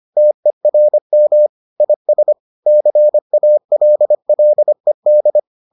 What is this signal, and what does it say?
Morse code, heard as a keyed tone, about 25 wpm, 600 Hz. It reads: TERM IS CALLED